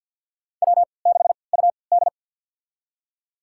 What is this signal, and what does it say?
Morse code, heard as a keyed tone, about 48 wpm, spaced as for 28 wpm, 700 Hz. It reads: W6UD
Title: USA Call Signs - 4 Characters - 3x Character Spacing - 50wpm